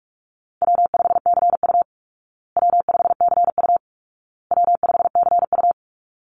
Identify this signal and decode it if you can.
Morse code, heard as a keyed tone, about 44 wpm, 700 Hz. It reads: W5CU W5CU W5CU
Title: Call Signs Repeated 2x - 45wpm